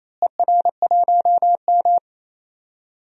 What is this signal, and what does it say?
Morse code, heard as a keyed tone, about 28 wpm, 700 Hz. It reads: ER1M
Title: DXCC-100 Call Signs - 4 Characters - Repeated 3x - 28wpm